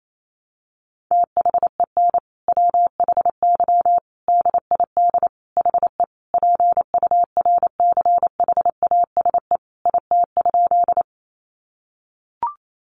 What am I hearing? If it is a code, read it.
Morse code, heard as a keyed tone, about 28 wpm, 700 Hz. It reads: THEN WHY DID HE PURCHASE IT?
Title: Sentences from Top 800 Words 28wpm